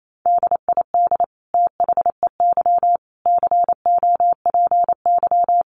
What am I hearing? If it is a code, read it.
Morse code, heard as a keyed tone, about 28 wpm, 700 Hz. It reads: DID THEY COPY
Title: Sentences from Top 1400 Words 28wpm